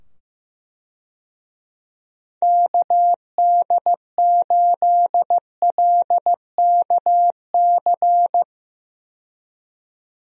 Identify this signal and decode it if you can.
Morse code, heard as a keyed tone, about 15 wpm, 700 Hz. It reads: KD8LKC
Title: USA Call Signs - 6 Characters - 15wpm